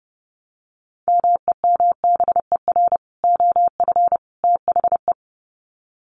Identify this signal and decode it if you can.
Morse code, heard as a keyed tone, about 30 wpm, 700 Hz. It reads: MEMBER OF THE
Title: Sets of 3 Words 30wpm